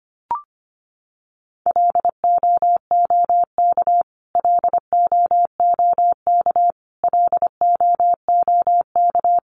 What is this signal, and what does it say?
Morse code, heard as a keyed tone, about 25 wpm, 700 Hz. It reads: LOOK LOOK LOOK
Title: Top 500 Words Repeated 25wpm